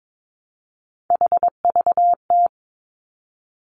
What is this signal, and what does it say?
Morse code, heard as a keyed tone, about 22 wpm, 700 Hz. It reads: HVT
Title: Three Challenging Letter-Number Combinations 22wpm